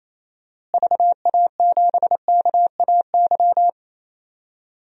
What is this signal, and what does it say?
Morse code, heard as a keyed tone, about 28 wpm, 700 Hz. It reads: VA7KAY
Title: DXCC-100 Call Signs - 6 Characters - Repeated 2x - 28wpm